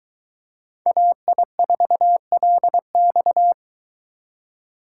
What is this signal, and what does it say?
Morse code, heard as a keyed tone, about 21 wpm, 700 Hz. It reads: AI4LX